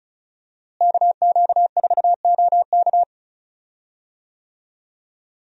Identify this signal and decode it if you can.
Morse code, heard as a keyed tone, about 36 wpm, 700 Hz. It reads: KQ4OK
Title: USA Call Signs - 5 Characters - Repeated 3x - 35wpm